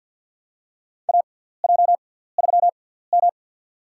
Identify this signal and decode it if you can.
Morse code, heard as a keyed tone, about 46 wpm, spaced as for 17 wpm, 700 Hz. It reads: AJ3M
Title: USA Call Signs - 6x Character Spacing - 50wpm